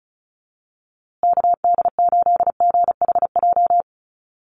Morse code, heard as a keyed tone, about 34 wpm, 700 Hz. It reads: KD8GHJ
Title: USA Call Signs Repeated 2x 35wpm